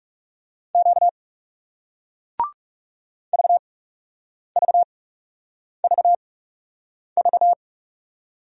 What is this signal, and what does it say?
Morse code, heard as a keyed tone, about 40 wpm, spaced as for 8 wpm, 700 Hz. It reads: Q VVVV